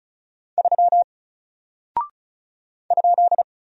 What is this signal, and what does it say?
Morse code, heard as a keyed tone, about 36 wpm, 700 Hz. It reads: 3 ?
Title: Single Character (Letter:Number:Punctuation:Prosign) Alt 35wpm